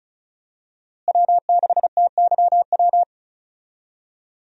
Morse code, heard as a keyed tone, about 34 wpm, 700 Hz. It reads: W6TYW
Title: USA Call Signs - 5 Characters - Repeated 2x - 35wpm